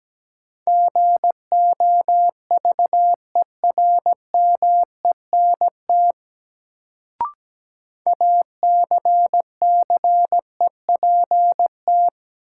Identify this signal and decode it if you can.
Morse code, heard as a keyed tone, about 17 wpm, 700 Hz. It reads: GOVERMENT ACCEPT
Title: Sentences from Top 1300 Words - New Word Drill - 17wpm